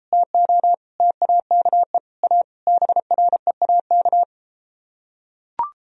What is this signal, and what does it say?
Morse code, heard as a keyed tone, about 33 wpm, 700 Hz. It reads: TO TAKE A BREAK